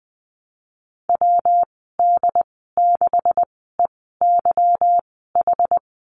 Morse code, heard as a keyed tone, about 20 wpm, spaced as for 14 wpm, 700 Hz. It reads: WD6EYH
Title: 179 Lesson 29g - Slash - Call Signs 20wpm